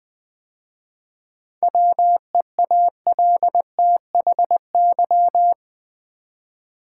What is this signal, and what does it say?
Morse code, heard as a keyed tone, about 20 wpm, 700 Hz. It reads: WEALTHY
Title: 7 Letter Words - First 3 Letters Given - 20wpm